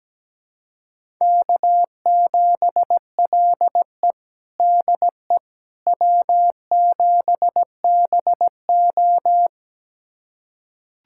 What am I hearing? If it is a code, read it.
Morse code, heard as a keyed tone, about 17 wpm, 700 Hz. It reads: K7LE DE W7BO
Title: USA Callsign DE Callsign - 4 Characters - 17wpm